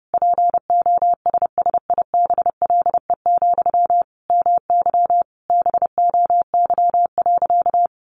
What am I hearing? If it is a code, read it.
Morse code, heard as a keyed tone, about 28 wpm, 700 Hz. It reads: POSSIBLE, MY BOY.
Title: Sentences from Top 800 Words 30wpm